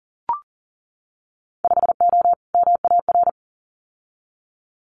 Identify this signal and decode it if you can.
Morse code, heard as a keyed tone, about 38 wpm, 700 Hz. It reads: 5O MAR
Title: Winter Field Day Exchanges Encore 40wpm